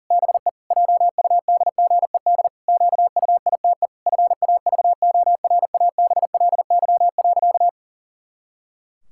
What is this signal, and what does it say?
Morse code, heard as a keyed tone, about 40 wpm, 700 Hz. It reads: BE JUDGED QUITE FAVORABLY.
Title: Pangram Challenge v2 - 40wpm